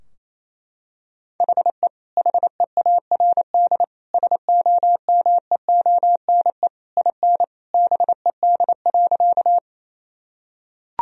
Morse code, heard as a keyed tone, about 28 wpm, 700 Hz. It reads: HE HEARD SOMEONE IN BED.
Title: Sentences from Top 1000 Words - Review Part 4 - 28wpm